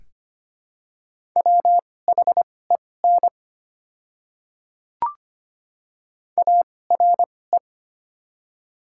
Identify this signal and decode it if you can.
Morse code, heard as a keyed tone, about 25 wpm, spaced as for 18 wpm, 690 Hz. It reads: WHEN ARE
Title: 2x Character Spacing - Top 100 Words - 25wpm